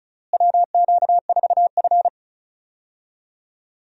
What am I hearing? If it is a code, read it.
Morse code, heard as a keyed tone, about 36 wpm, 700 Hz. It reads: WQ4F
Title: USA Call Signs - 4 Characters - Repeated 2x - 35wpm